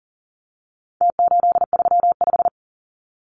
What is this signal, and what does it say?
Morse code, heard as a keyed tone, about 40 wpm, 700 Hz. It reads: T835
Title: Numbers - Contest Style - 4-Digit 40wpm